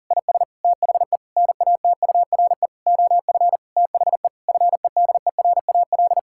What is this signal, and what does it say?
Morse code, heard as a keyed tone, about 40 wpm, 700 Hz. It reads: IS THE NATURE OF THE FEDERAL
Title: Sentences from Top 1000 Words - Review Part 5 - 40wpm